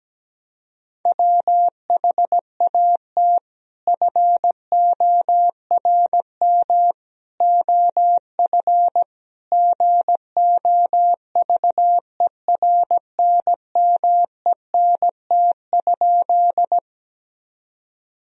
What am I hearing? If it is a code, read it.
Morse code, heard as a keyed tone, about 17 wpm, 700 Hz. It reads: WHAT FORM OF GOVERNMENT?
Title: Sentences from Top 1000 Words - Review Part 3 - 17wpm